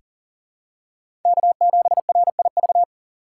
Morse code, heard as a keyed tone, about 40 wpm, 700 Hz. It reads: K7RIV
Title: USA Call Signs - 5 Characters - 3x Character Spacing - 40wpm